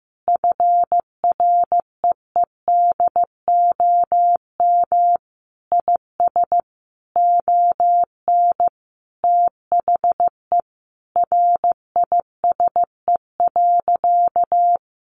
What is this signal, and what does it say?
Morse code, heard as a keyed tone, about 15 wpm, 700 Hz. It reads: FREEDOM IS ON THE RISE.
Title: Sentences from Top 1300 Words 15wpm